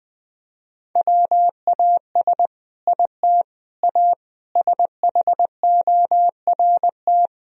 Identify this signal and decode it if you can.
Morse code, heard as a keyed tone, about 20 wpm, 700 Hz. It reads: WAS IT A SHORT
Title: Sentences from Top 1000 Words - Review Part 4 - 20wpm